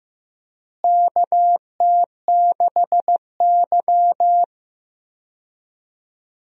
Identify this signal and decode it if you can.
Morse code, heard as a keyed tone, about 15 wpm, 700 Hz. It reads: KT6Y
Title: USA Call Signs - 4 Characters - Repeated 3x - 15wpm